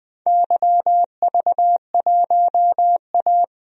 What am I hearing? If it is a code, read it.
Morse code, heard as a keyed tone, about 20 wpm, 700 Hz. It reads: YV1A
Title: DXCC-100 Call Signs - 4 Characters - 20wpm